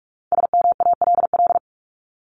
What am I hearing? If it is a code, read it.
Morse code, heard as a keyed tone, about 44 wpm, 700 Hz. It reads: SMALL
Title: Top 200 Words 45wpm